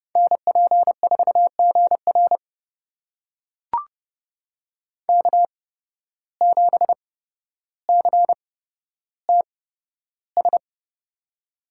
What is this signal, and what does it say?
Morse code, heard as a keyed tone, about 32 wpm, spaced as for 13 wpm, 700 Hz. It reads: NP4GR K7CTS
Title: USA Call Signs - 5 Characters - 8x Character Spacing - 30wpm